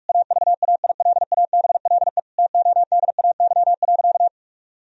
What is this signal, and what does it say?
Morse code, heard as a keyed tone, about 41 wpm, 700 Hz. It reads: AVAILABLE TODAY.
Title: Sentences from Top 1000 Words - Review Part 3 - 45wpm